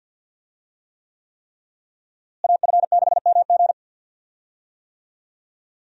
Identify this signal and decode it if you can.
Morse code, heard as a keyed tone, about 46 wpm, 700 Hz. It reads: AF6MG
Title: USA Call Signs - 5 Characters - Repeated 3x - 50wpm